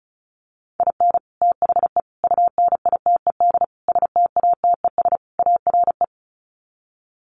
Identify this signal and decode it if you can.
Morse code, heard as a keyed tone, about 34 wpm, 700 Hz. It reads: IN THE UNITED STATES ARE
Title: Sets of 5 Words Encore 35wpm